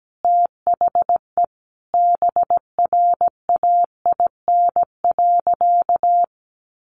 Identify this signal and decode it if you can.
Morse code, heard as a keyed tone, about 17 wpm, 700 Hz. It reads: THE BRAIN.
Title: Sentences from Top 1100 Words 17wpm